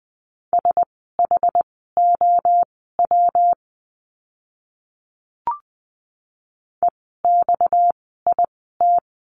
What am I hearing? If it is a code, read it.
Morse code, heard as a keyed tone, about 20 wpm, spaced as for 15 wpm, 700 Hz. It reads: SHOW EXIT
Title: 230 Lesson 37f - Letter x - Words 20wpm